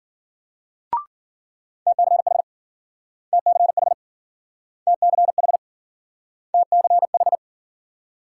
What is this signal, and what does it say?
Morse code, heard as a keyed tone, about 52 wpm, 700 Hz. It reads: TCH TCH TCH TCH